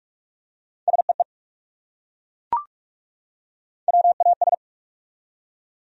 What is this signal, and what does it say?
Morse code, heard as a keyed tone, about 42 wpm, 700 Hz. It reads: SEE WAS
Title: Top 100 Words 45wpm